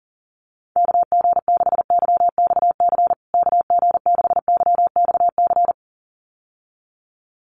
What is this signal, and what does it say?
Morse code, heard as a keyed tone, about 40 wpm, 700 Hz. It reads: KG6YXC KG6YXC
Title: USA Call Signs Repeated 2x 40wpm